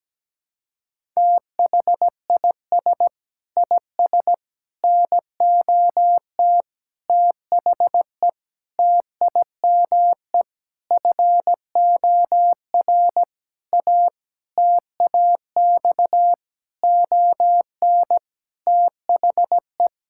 Morse code, heard as a keyed tone, about 17 wpm, 700 Hz. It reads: THIS IS NOT THE TIME FOR A TAX ON THE
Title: Sentences from Top 1000 Words - Review Part 5 - 17wpm